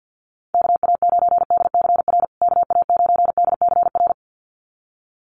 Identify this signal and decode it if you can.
Morse code, heard as a keyed tone, about 50 wpm, 700 Hz. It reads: KA9DCR KA9DCR
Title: USA Call Signs Repeated 2x 50wpm